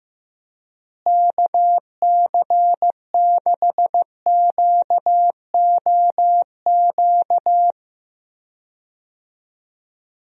Morse code, heard as a keyed tone, about 15 wpm, 700 Hz. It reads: KC6QOQ
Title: USA Call Signs - 6 Characters - Repeated 3x - 15wpm